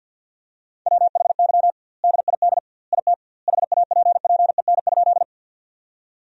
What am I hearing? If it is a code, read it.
Morse code, heard as a keyed tone, about 50 wpm, 700 Hz. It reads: WHY DID IT HAPPEN?
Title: 2x Word-Spacing - Sentences from Top 400 Words - 50wpm